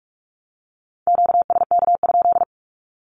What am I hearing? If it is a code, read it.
Morse code, heard as a keyed tone, about 44 wpm, 700 Hz. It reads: QSK?
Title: Q-Codes 45wpm